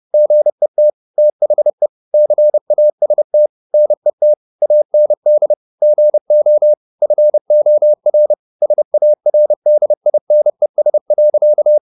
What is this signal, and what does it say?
Morse code, heard as a keyed tone, about 30 wpm, 600 Hz. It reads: GET THE CAST NET AND GO FOR SARDINES.
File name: the old man and the sea / the old man and the sea 1 30wpm